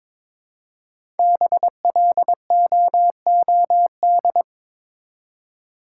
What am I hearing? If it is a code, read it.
Morse code, heard as a keyed tone, about 22 wpm, 700 Hz. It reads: BLOOD
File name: 5 Letter Words - First 3 Letters Given - 22wpm